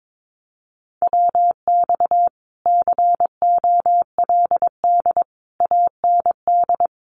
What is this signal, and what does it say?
Morse code, heard as a keyed tone, about 22 wpm, 700 Hz. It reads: WX COLD AND
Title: Common QSO Phrases Encore 22wpm